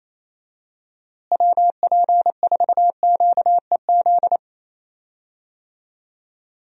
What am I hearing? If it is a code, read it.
Morse code, heard as a keyed tone, about 28 wpm, 700 Hz. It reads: WP4QEZ
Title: USA Call Signs - 6 Characters - Repeated 2x - 28wpm